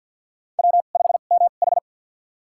Morse code, heard as a keyed tone, about 52 wpm, spaced as for 37 wpm, 700 Hz. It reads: W5MH
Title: USA Call Signs - 4 Characters - 2x Character Spacing - 50wpm